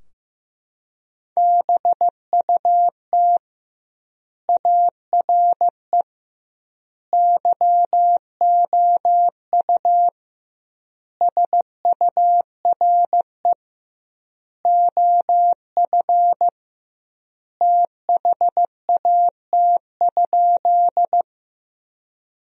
Morse code, heard as a keyed tone, about 15 wpm, 700 Hz. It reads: BUT ARE YOU SURE OF THAT?
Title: 2x Word-Spacing - Sentences from Top 300 Words - 15wpm